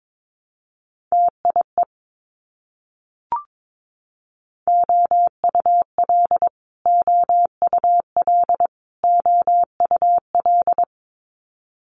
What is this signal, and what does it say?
Morse code, heard as a keyed tone, about 22 wpm, 700 Hz. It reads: TIE OUL OUL OUL
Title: Multi-letter Phonemes Repeated 3x 22wpm